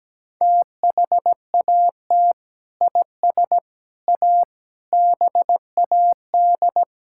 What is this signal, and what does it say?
Morse code, heard as a keyed tone, about 17 wpm, 700 Hz. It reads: THAT IS A BAD
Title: Sentences from Top 1600 Words 17wpm